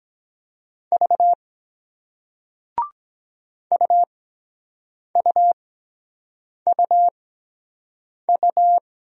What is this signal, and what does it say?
Morse code, heard as a keyed tone, about 22 wpm, spaced as for 7 wpm, 700 Hz. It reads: V UUUU